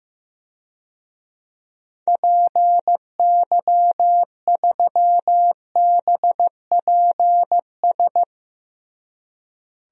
Morse code, heard as a keyed tone, about 15 wpm, 700 Hz. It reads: PY3BPS
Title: DXCC-100 Call Signs - 6 Characters - Repeated 2x - 15wpm